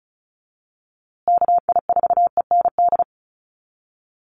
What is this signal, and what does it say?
Morse code, heard as a keyed tone, about 34 wpm, 700 Hz. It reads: KI4END
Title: USA Call Signs Repeated 3x 35wpm